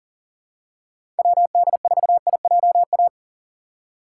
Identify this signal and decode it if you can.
Morse code, heard as a keyed tone, about 40 wpm, 700 Hz. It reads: WD4IJA
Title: USA Call Signs - 6 Characters - 4x Character Spacing - 40wpm